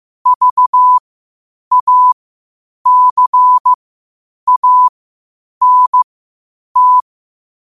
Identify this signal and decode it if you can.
Morse code, heard as a keyed tone, about 15 wpm, spaced as for 8 wpm, 1000 Hz. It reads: VACANT
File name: qotdfiles / 8wpm